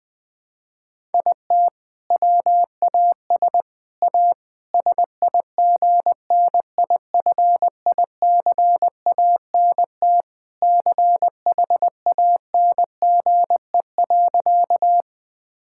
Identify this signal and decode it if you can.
Morse code, heard as a keyed tone, about 21 wpm, 700 Hz. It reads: IT WAS A SIGNIFICANT CHANGE.
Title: Sentences from Top 1000 Words - Review Part 6 - 20wpm